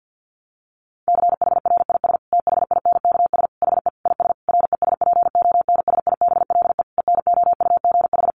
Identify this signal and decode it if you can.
Morse code, heard as a keyed tone, about 50 wpm, 700 Hz. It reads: CHRIS THINKS HE IS RESPONSIBLE ENOUGH
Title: Additional Easy Phrases 50wpm